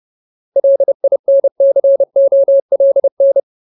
Morse code, heard as a keyed tone, about 30 wpm, 550 Hz. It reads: LINCOLN
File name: Ultimate Word List / 30WPM Ultimate